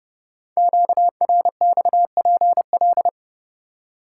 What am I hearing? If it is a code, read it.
Morse code, heard as a keyed tone, about 28 wpm, 700 Hz. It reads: QRXPL
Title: Code Groups - 5 Letters - 30wpm